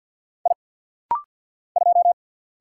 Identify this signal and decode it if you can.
Morse code, heard as a keyed tone, about 48 wpm, 700 Hz. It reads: I 2
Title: Single Letter-Number - Rapid-Fire 50wpm